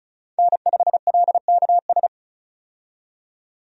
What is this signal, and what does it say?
Morse code, heard as a keyed tone, about 36 wpm, 700 Hz. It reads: N5LKS
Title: USA Call Signs - 5 Characters - 35wpm